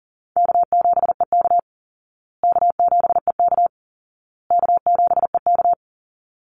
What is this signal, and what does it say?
Morse code, heard as a keyed tone, about 40 wpm, 700 Hz. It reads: K7EK K7EK K7EK
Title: Easy Call Signs Repeated 3x - 40wpm